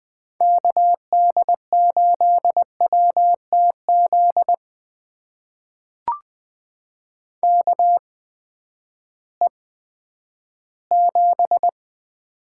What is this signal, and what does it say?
Morse code, heard as a keyed tone, about 20 wpm, spaced as for 11 wpm, 700 Hz. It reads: KD8WTZ KE7
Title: USA Call Signs - 8x Character Spacing - 20wpm